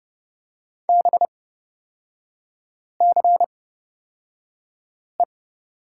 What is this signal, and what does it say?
Morse code, heard as a keyed tone, about 30 wpm, spaced as for 5 wpm, 700 Hz. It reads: BCE
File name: Lesson 36e - BT - Three Characters 30wpm